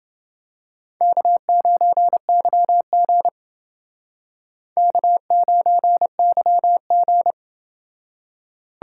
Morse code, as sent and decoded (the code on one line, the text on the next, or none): -.- ----. -.-- --. / -.- ----. -.-- --.
K9YG K9YG